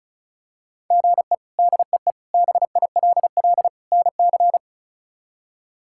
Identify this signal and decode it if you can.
Morse code, heard as a keyed tone, about 34 wpm, 700 Hz. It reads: GE DEE BILL NC
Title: K1USN SST QSO Practice - 35wpm